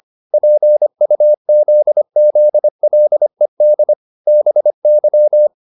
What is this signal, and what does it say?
Morse code, heard as a keyed tone, about 25 wpm, 600 Hz. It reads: PUZZLED BY